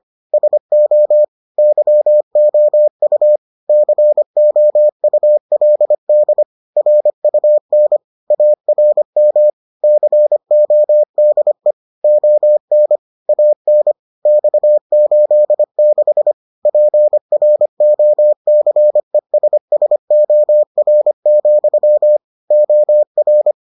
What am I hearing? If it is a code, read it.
Morse code, heard as a keyed tone, about 25 wpm, 600 Hz. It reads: SO YOU COULD RUN ARM CODE ON AN X86 PROCESSOR, OR